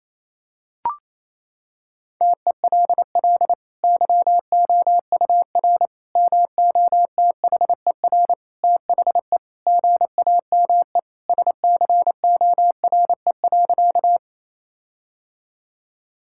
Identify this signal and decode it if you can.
Morse code, heard as a keyed tone, about 28 wpm, 700 Hz. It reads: TELL YOUR MOTHER THE GAME SCORE.
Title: Sentences from Top 1600 Words 28wpm